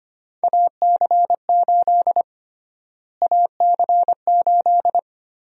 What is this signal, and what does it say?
Morse code, heard as a keyed tone, about 25 wpm, 700 Hz. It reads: AC8 AC8
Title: USA Call Sign Prefixes - Repeated 2x - 25wpm